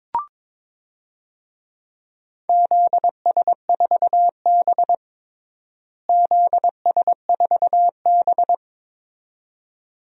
Morse code, heard as a keyed tone, about 22 wpm, 700 Hz. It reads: ZS4B ZS4B
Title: DXCC-100 Call Signs - 4 Characters - Repeated 2x - 22wpm